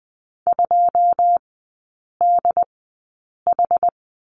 Morse code, heard as a keyed tone, about 20 wpm, spaced as for 8 wpm, 700 Hz. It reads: 2DH
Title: 127 Lesson 21e - Number 6 - Three Characters 20wpm